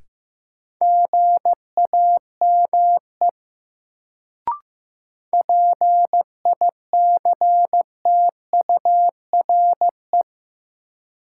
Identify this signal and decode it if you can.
Morse code, heard as a keyed tone, about 15 wpm, 700 Hz. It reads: GAME PICTURE
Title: Sentences from Top 500 Words - New Word Drill - 15wpm